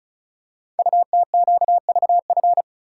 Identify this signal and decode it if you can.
Morse code, heard as a keyed tone, about 36 wpm, 700 Hz. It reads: UTQVF
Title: Code Groups - 5 Letters - 35wpm